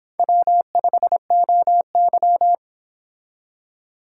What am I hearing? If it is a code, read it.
Morse code, heard as a keyed tone, about 26 wpm, 700 Hz. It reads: W5OY